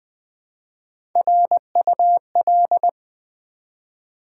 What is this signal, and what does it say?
Morse code, heard as a keyed tone, about 20 wpm, 700 Hz. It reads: RUL
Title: Build A Word - 1 Letter At A Time - 4-Letter Words - 20wpm